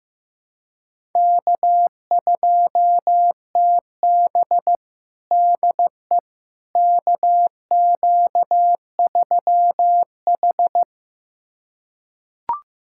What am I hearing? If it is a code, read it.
Morse code, heard as a keyed tone, about 15 wpm, 700 Hz. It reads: K2TB DE KQ3H
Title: USA Callsign DE Callsign - 4 Characters - 15wpm